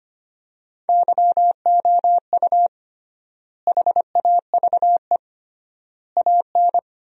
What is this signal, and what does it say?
Morse code, heard as a keyed tone, about 25 wpm, 700 Hz. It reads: YOU HAVE AN
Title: 3x Word-Spacing - Sentences from Top 400 Words - 25wpm